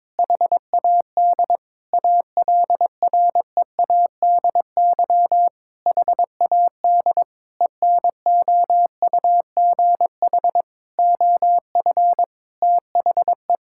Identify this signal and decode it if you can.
Morse code, heard as a keyed tone, about 22 wpm, 700 Hz. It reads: HAD ALREADY HAD ENOUGH OF THE
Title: Sentences from Top 1600 Words 22wpm